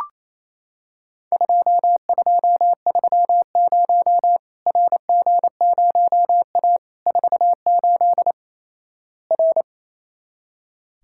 Morse code, heard as a keyed tone, about 28 wpm, 700 Hz. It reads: 2230 RG0A 48 R
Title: Worked All Europe DX Contest - QTC Exchange - 28wpm